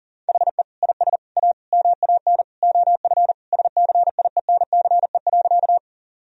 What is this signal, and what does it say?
Morse code, heard as a keyed tone, about 37 wpm, 700 Hz. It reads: HE IS A MAN OF SCIENCE.
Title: Sentences from Top 1000 Words - Review Part 2 - 40wpm